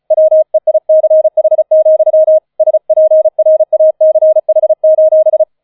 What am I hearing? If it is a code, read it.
Morse code, heard as a keyed tone, about 36 wpm, 630 Hz. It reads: WEICH, SPRACH: